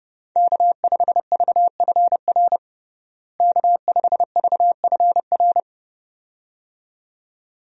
Morse code, heard as a keyed tone, about 30 wpm, 700 Hz. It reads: K5VFR K5VFR
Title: USA Call Signs - 5 Characters - Repeated 2x - 30wpm